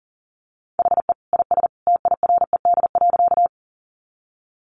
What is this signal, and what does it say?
Morse code, heard as a keyed tone, about 38 wpm, 700 Hz. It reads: HE IS TIRED.
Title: Additional Easy Phrases 40wpm